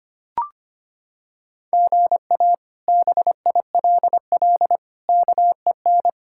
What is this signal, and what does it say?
Morse code, heard as a keyed tone, about 25 wpm, 700 Hz. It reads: GA BILL KEN
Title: K1USN SST QSO Practice - 25wpm